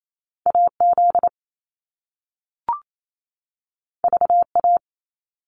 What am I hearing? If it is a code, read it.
Morse code, heard as a keyed tone, about 28 wpm, 700 Hz. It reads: AZ VA
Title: US States 28wpm